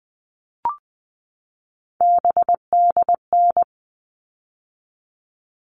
Morse code, heard as a keyed tone, about 20 wpm, 700 Hz. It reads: BDN
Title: All Cut Numbers - 3-Digit 20wpm